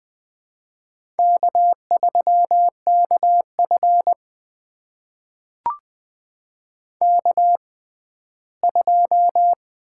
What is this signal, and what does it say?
Morse code, heard as a keyed tone, about 20 wpm, spaced as for 14 wpm, 700 Hz. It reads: K3KF K2
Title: USA Call Signs - 4 Characters - 6x Character Spacing - 20wpm